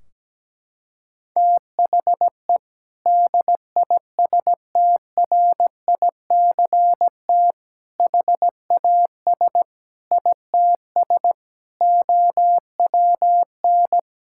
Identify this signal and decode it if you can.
Morse code, heard as a keyed tone, about 17 wpm, 700 Hz. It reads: THE DISTRICT HAS ITS OWN
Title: Sentences from Top 1600 Words 17wpm